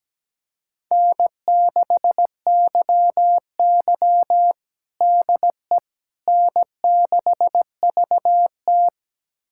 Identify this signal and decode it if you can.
Morse code, heard as a keyed tone, about 17 wpm, 700 Hz. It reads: N6YY DE N6VT
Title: USA Callsign DE Callsign - 4 Characters - 17wpm